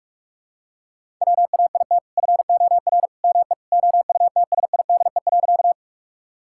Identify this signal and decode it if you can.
Morse code, heard as a keyed tone, about 46 wpm, 700 Hz. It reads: WAIT FOR ME OUTSIDE.
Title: Sentences from Top 1000 Words - Review Part 6 - 45wpm